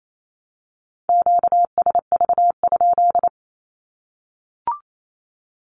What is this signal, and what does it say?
Morse code, heard as a keyed tone, about 28 wpm, 700 Hz. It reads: QSV?
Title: Q-Codes 28wpm